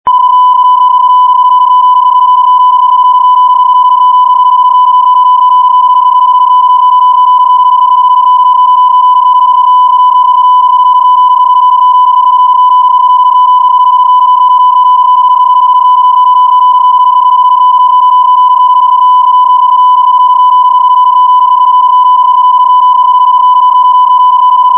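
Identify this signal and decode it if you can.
PSK31, about 31.25 baud.
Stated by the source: PSK31 CQ call, keyboard-to-keyboard mode with 31.25 baud phase shift keying and Varicode.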